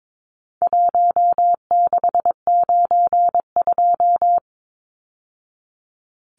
Morse code, heard as a keyed tone, about 22 wpm, 700 Hz. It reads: 1692
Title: Numbers - 4-Digit 22wpm